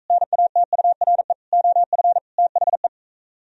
Morse code, heard as a keyed tone, about 45 wpm, 700 Hz. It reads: NATURE OF THE